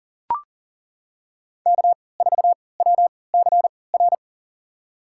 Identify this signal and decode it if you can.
Morse code, heard as a keyed tone, about 40 wpm, spaced as for 23 wpm, 700 Hz. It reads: K4WCR
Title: USA Call Signs - 5 Characters - 3x Character Spacing - 40wpm